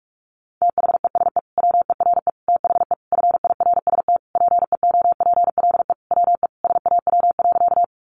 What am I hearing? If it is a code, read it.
Morse code, heard as a keyed tone, about 44 wpm, 700 Hz. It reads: THESE WERE THE FIRST PEOPLE WE SAW.
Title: Sentences from Top 300 Words 45wpm